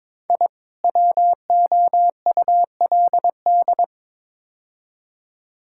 Morse code, heard as a keyed tone, about 22 wpm, 700 Hz. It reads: I WOULD
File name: practice / 22wpm / Build A Phrase - 5-Word Phrases - 22wpm